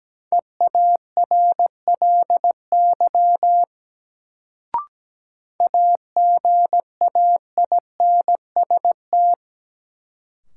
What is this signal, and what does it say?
Morse code, heard as a keyed tone, about 17 wpm, 700 Hz. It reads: EARLY AGAINST
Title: Sentences from Top 400 Words - New Word Drill - 17wpm